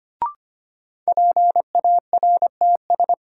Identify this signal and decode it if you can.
Morse code, heard as a keyed tone, about 25 wpm, 700 Hz. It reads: PARTS
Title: Sets of 3 Words - New Word Drill - 25wpm